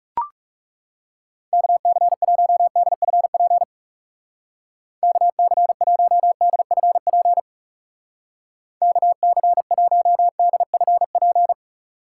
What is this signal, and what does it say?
Morse code, heard as a keyed tone, about 40 wpm, 700 Hz. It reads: KC1DFP KC1DFP KC1DFP